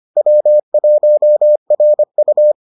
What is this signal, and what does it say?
Morse code, heard as a keyed tone, about 25 wpm, 600 Hz. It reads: W1RU